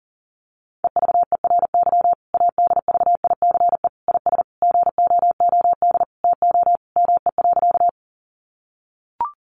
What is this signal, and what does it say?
Morse code, heard as a keyed tone, about 40 wpm, 700 Hz. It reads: EVERY ADVICE IS GOOD TO ME.
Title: Sentences from Top 1100 Words 40wpm